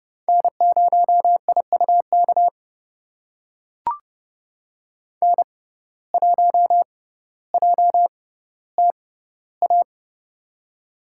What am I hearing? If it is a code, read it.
Morse code, heard as a keyed tone, about 30 wpm, spaced as for 15 wpm, 700 Hz. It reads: N0IUK N1JTA
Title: USA Call Signs - 5 Characters - 6x Character Spacing - 30wpm